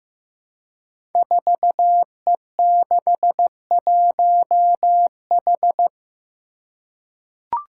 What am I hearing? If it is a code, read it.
Morse code, heard as a keyed tone, about 15 wpm, 700 Hz. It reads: 4E61H
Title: Five Challenging Letter-Number Combinations 15wpm